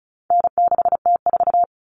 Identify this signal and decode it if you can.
Morse code, heard as a keyed tone, about 34 wpm, 700 Hz. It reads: N6T4
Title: All Cut Numbers - 4-Digit 35wpm